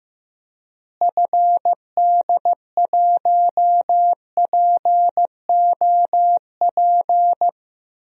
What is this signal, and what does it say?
Morse code, heard as a keyed tone, about 15 wpm, 700 Hz. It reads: FD1POP
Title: DXCC-100 Call Signs - 6 Characters - Repeated 2x - 15wpm